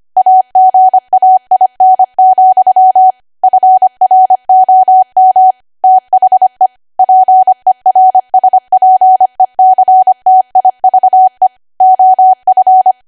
Morse code, heard as a keyed tone, about 25 wpm, 750 Hz. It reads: AGAIN, FROM THE PERSPECTIVE OF
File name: Archive / 25 WPM / 25WPM